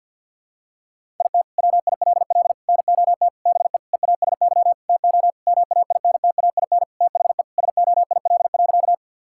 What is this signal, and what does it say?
Morse code, heard as a keyed tone, about 50 wpm, 700 Hz. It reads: IT WILL NOT BE EASY TO MAINTAIN THE SOIL.
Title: Sentences from Top 1000 Words - Review Part 5 - 50wpm